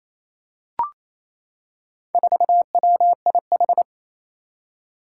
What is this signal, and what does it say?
Morse code, heard as a keyed tone, about 27 wpm, 700 Hz. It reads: 4WIH
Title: Four Challenging Letter-Number Combinations 28wpm